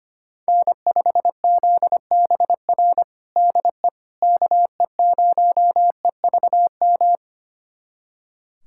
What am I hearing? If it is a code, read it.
Morse code, heard as a keyed tone, about 25 wpm, 700 Hz. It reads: N5ZBR DE KE0EVM
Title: USA Callsign DE Callsign 25wpm